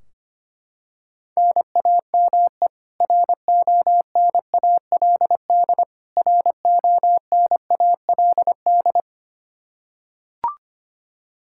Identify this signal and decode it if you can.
Morse code, heard as a keyed tone, about 25 wpm, 700 Hz. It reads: NAME RONALD RONALD
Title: Practice QSO - Exchange 1 - Beginning - 25wpm